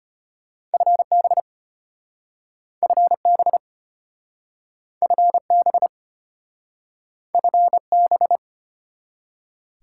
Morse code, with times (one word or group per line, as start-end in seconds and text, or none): 0.7-1.4 FB
2.8-3.6 FB
5.0-5.9 FB
7.3-8.4 FB